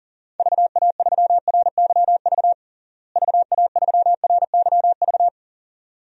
Morse code, heard as a keyed tone, about 40 wpm, 700 Hz. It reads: VA3RYV VA3RYV
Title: DXCC-100 Call Signs - 6 Characters - Repeated 2x - 40wpm